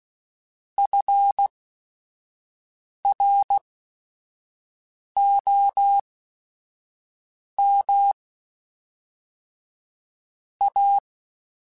Morse code, heard as a keyed tone, about 16 wpm, spaced as for 5 wpm, 800 Hz. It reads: FROM A